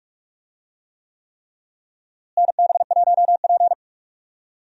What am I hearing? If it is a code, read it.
Morse code, heard as a keyed tone, about 47 wpm, 700 Hz. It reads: NB1P
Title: USA Call Signs - 4 Characters - Repeated 3x - 45wpm